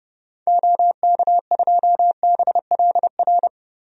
Morse code, heard as a keyed tone, about 30 wpm, 700 Hz. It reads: OK2BLR
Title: DXCC-100 Call Signs - 6 Characters - Repeated 3x - 30wpm